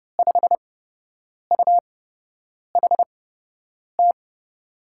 Morse code, heard as a keyed tone, about 31 wpm, spaced as for 8 wpm, 700 Hz. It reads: 5UHT
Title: USA Call Signs - 6 Characters - 8x Character Spacing - 30wpm